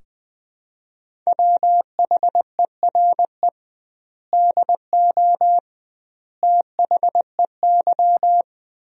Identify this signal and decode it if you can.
Morse code, heard as a keyed tone, about 20 wpm, 700 Hz. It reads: WHERE DO THEY
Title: 2x Word-Spacing - Sentences from Top 200 Words - 20wpm